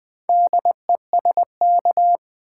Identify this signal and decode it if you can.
Morse code, heard as a keyed tone, about 20 wpm, 700 Hz. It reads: DESK
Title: Four Letter Words - 20wpm